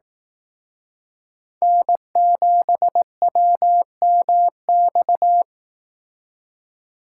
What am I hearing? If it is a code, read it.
Morse code, heard as a keyed tone, about 18 wpm, 700 Hz. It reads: N7WMX
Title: Speed Racing - USA 1x3 Call Signs 15wpm